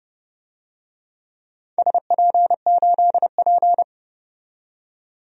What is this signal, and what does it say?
Morse code, heard as a keyed tone, about 30 wpm, 700 Hz. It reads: SP8P
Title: DXCC-100 Call Signs - 4 Characters - 30wpm